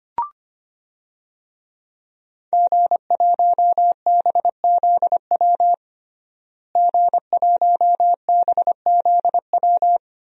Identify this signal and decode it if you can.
Morse code, heard as a keyed tone, about 25 wpm, 700 Hz. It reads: G1BZW G1BZW
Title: DXCC-100 Call Signs - 5 Characters - Repeated 3x - 25wpm